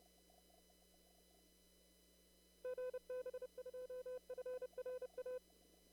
Morse code, heard as a keyed tone, about 30 wpm, 510 Hz. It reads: GB2FRA
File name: CW / kt5j wpx cw-op-k5pi